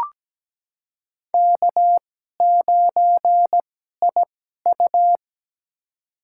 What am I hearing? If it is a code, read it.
Morse code, heard as a keyed tone, about 17 wpm, spaced as for 12 wpm, 700 Hz. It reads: K9IU
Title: USA Call Signs - 4 Characters - 2x Character Spacing - 17wpm